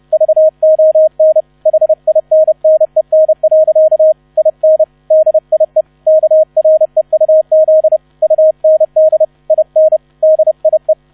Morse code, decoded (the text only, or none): VON HINNEN. IN DIE KREUZ UND IN DIE